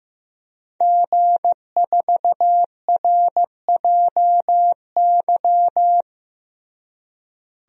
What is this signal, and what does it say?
Morse code, heard as a keyed tone, about 15 wpm, 700 Hz. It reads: G4RJY
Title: DXCC-100 Call Signs - 5 Characters - Repeated 3x - 15wpm